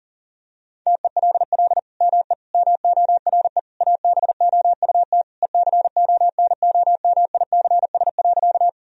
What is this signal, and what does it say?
Morse code, heard as a keyed tone, about 40 wpm, 700 Hz. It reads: TELL ME MORE ABOUT ECONOMICS.
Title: Sentences from Top 1000 Words - Review Part 3 - 40wpm